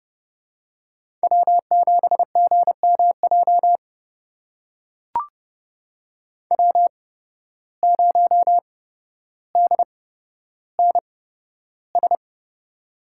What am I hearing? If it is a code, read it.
Morse code, heard as a keyed tone, about 32 wpm, spaced as for 13 wpm, 700 Hz. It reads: W7GMJ W0DNS
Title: USA Call Signs - 8x Character Spacing - 30wpm